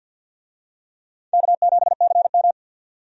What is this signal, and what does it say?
Morse code, heard as a keyed tone, about 51 wpm, 700 Hz. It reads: K7CM
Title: USA Call Signs - 4 Characters - 50wpm